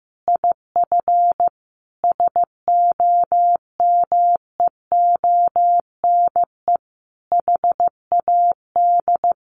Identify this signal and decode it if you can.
Morse code, heard as a keyed tone, about 15 wpm, 700 Hz. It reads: IF SOMEONE HAD
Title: A Sound Of Thunder 15wpm